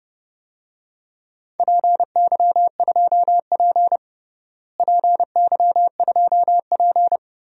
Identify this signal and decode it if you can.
Morse code, heard as a keyed tone, about 30 wpm, 700 Hz. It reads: PY2P PY2P
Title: DXCC-100 Call Signs - 4 Characters - Repeated 3x - 30wpm